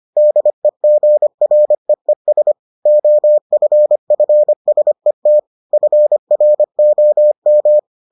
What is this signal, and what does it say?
Morse code, heard as a keyed tone, about 25 wpm, 600 Hz. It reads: DEGREES OFFSET FROM